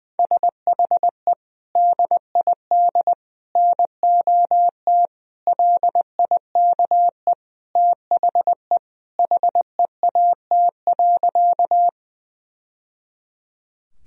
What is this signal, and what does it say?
Morse code, heard as a keyed tone, about 20 wpm, 700 Hz. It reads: SHE DID NOT LIKE THE HEAT.
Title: Sentences from Top 1000 Words - Review Part 3 - 20wpm